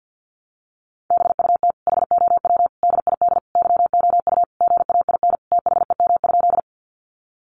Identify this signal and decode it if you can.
Morse code, heard as a keyed tone, about 50 wpm, 700 Hz. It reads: BUT HOW DID YOU GAIN THEM?
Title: Sentences from Top 1000 Words 50wpm